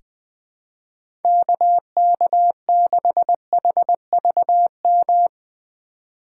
Morse code, heard as a keyed tone, about 21 wpm, 700 Hz. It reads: KK6HVM
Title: USA Call Signs - 6 Characters - 2x Character Spacing - 20wpm